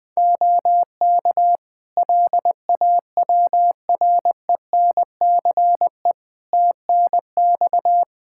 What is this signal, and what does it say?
Morse code, heard as a keyed tone, about 20 wpm, 700 Hz. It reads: OK LAWRENCE TNX
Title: Practice QSO - Exchange 4 - Ending - 20wpm